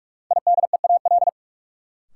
Morse code, heard as a keyed tone, about 45 wpm, 700 Hz. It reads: IDEAL
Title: Sentences from Top 1400 Words - New Word Drill Alt - 45wpm